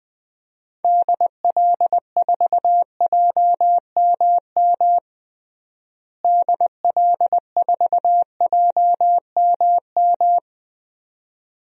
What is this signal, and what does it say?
Morse code, heard as a keyed tone, about 21 wpm, 700 Hz. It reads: DL4JMM DL4JMM
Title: DXCC-100 Call Signs - 6 Characters - Repeated 2x - 20wpm